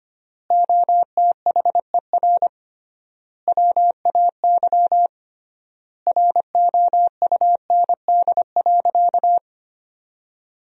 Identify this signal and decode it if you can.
Morse code, heard as a keyed tone, about 25 wpm, 700 Hz. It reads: OTHER WAY ROUND.
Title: 3x Word-Spacing - Sentences from Top 200 Words - 25wpm